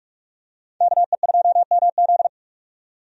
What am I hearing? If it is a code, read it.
Morse code, heard as a keyed tone, about 44 wpm, 700 Hz. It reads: KE2MZ
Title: USA Call Signs - 5 Characters - 45wpm